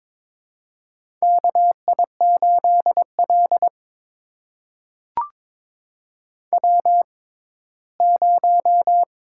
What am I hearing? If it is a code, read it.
Morse code, heard as a keyed tone, about 22 wpm, spaced as for 15 wpm, 700 Hz. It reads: KI8L W0
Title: USA Call Signs - 4 Characters - 6x Character Spacing - 22wpm